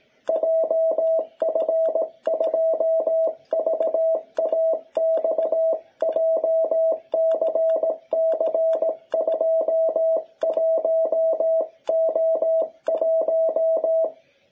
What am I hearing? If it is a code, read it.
Morse code, heard as a keyed tone, about 17 wpm, 660 Hz. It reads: JF2VAX JCC21O1
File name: jh8cbh / 2021cw